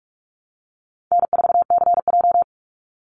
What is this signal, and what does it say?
Morse code, heard as a keyed tone, about 44 wpm, 700 Hz. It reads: N4CJ
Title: Call Signs - 45wpm